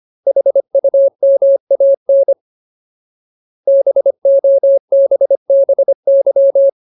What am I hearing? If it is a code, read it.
Morse code, heard as a keyed tone, about 25 wpm, 550 Hz. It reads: HUMAN BOBBY
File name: Ultimate Word List / 25WPM Ultimate